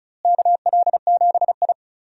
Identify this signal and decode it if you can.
Morse code, heard as a keyed tone, about 34 wpm, 700 Hz. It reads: KL7I
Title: DXCC-100 Call Signs - 4 Characters - Repeated 3x - 35wpm